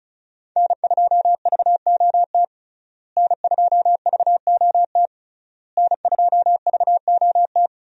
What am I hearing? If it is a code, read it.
Morse code, heard as a keyed tone, about 36 wpm, 700 Hz. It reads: N2VOT N2VOT N2VOT
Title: USA Call Signs - 5 Characters - Repeated 3x - 35wpm